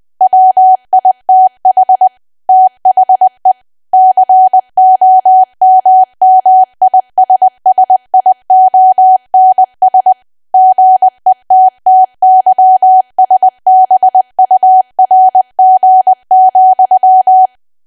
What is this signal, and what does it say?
Morse code, heard as a keyed tone, about 20 wpm, 750 Hz. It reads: WITH THE COMMISSIONS GETTYSBURG,
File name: cwa / W1AW-Level3 / 20WPM